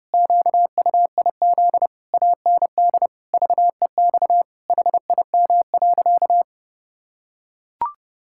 Morse code, heard as a keyed tone, about 32 wpm, 700 Hz. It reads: QUIZ AND VEX HIM.
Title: Pangram Challenge v2 - 30wpm